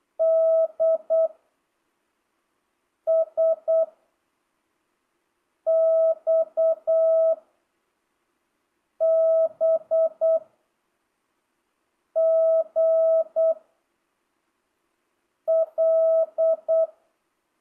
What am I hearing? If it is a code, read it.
Morse code, heard as a keyed tone, about 8 wpm, 660 Hz. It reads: D S X B G L
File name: jh8cbh / 2025cw